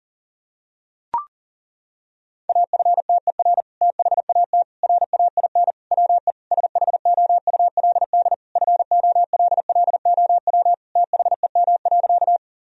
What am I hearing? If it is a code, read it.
Morse code, heard as a keyed tone, about 37 wpm, 700 Hz. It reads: AFTER THAT RAIN WE SHOULD FOLLOW THEM.
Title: Sentences from Top 1000 Words - Review Part 1 - 40wpm